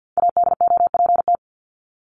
Morse code, heard as a keyed tone, about 50 wpm, 700 Hz. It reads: ADOPT
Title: ED Suffix - Top 100 Words with Base Words 50wpm